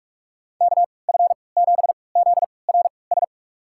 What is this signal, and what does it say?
Morse code, heard as a keyed tone, about 42 wpm, spaced as for 25 wpm, 700 Hz. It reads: KF7ZRS
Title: USA Call Signs - 6 Characters - 3x Character Spacing - 45wpm